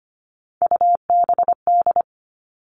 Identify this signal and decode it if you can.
Morse code, heard as a keyed tone, about 25 wpm, 700 Hz. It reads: UBD
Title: All Cut Numbers - 3-Digit 25wpm